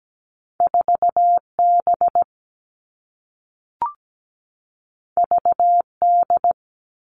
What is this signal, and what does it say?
Morse code, heard as a keyed tone, about 17 wpm, 700 Hz. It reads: 4B VD
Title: All Cut Numbers - 2-Digit 17wpm